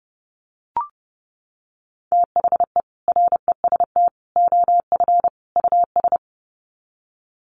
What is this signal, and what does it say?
Morse code, heard as a keyed tone, about 30 wpm, 700 Hz. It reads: THE REST OF US
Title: Sets of 4 Words 30wpm